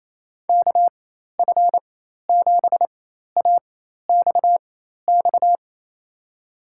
Morse code, heard as a keyed tone, about 28 wpm, spaced as for 13 wpm, 700 Hz. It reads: KF7AXX
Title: USA Call Signs - 6 Characters - 4x Character Spacing - 28wpm